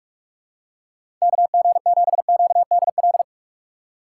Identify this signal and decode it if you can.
Morse code, heard as a keyed tone, about 46 wpm, 700 Hz. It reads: K G 7 Q D L
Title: USA Call Signs - 6 Characters - 6x Character Spacing - 45wpm